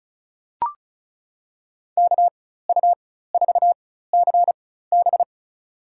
Morse code, heard as a keyed tone, about 36 wpm, spaced as for 16 wpm, 700 Hz. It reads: KU4CB
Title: USA Call Signs - 5 Characters - 4x Character Spacing - 35wpm